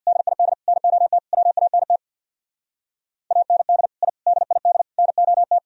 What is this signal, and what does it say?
Morse code, heard as a keyed tone, about 65 wpm, 700 Hz. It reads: DID NOT WANT AND I DID NOT